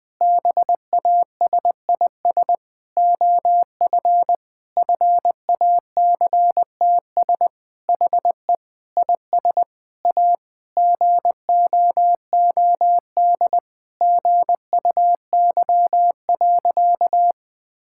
Morse code, heard as a keyed tone, about 20 wpm, 700 Hz. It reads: BASIS OF FACTS HE IS A GOOD GUY.
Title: Sentences with 3-Word Phrases - 20wpm